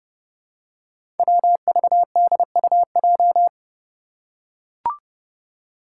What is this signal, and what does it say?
Morse code, heard as a keyed tone, about 30 wpm, 700 Hz. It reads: WVDUJ
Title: Five Challenging Letter-Number Combinations 30wpm